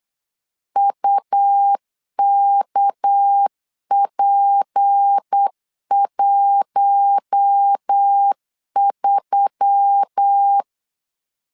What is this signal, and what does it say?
Morse code, heard as a keyed tone, about 8 wpm, 800 Hz. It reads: UKP13
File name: LEZIONI / 6 PKX0 / 6 PK 8wpm 3003